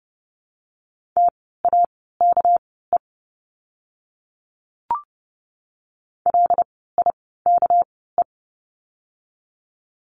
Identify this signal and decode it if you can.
Morse code, heard as a keyed tone, about 30 wpm, spaced as for 17 wpm, 700 Hz. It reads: TAKE LIKE
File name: course / 30:10wpm / 237 Lesson 38f - Letter q - Words 30wpm